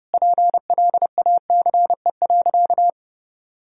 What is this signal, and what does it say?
Morse code, heard as a keyed tone, about 30 wpm, 700 Hz. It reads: PLACE.
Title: Sentences from Top 1000 Words - Review Part 1 - 30wpm